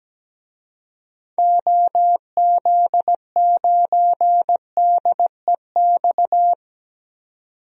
Morse code, heard as a keyed tone, about 17 wpm, 700 Hz. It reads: OZ9DEX
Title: DXCC-100 Call Signs - 6 Characters - 17wpm